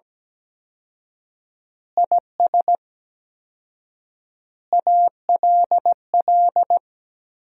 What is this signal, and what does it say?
Morse code, heard as a keyed tone, about 17 wpm, 700 Hz. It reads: IS ALL
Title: 4x Word-Spacing - Sentences from Top 100 Words - 17wpm